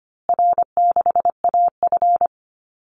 Morse code, heard as a keyed tone, about 25 wpm, 700 Hz. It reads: R6AF
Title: Call Signs Repeated 2x - 25wpm